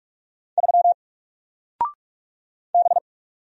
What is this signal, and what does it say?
Morse code, heard as a keyed tone, about 44 wpm, 700 Hz. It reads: 3 B
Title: Single Letter-Number 45wpm